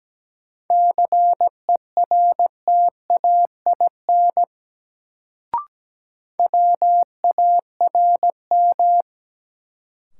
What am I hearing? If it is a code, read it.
Morse code, heard as a keyed tone, about 17 wpm, 700 Hz. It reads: CERTAIN WARM
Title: Sentences from Top 500 Words - New Word Drill - 17wpm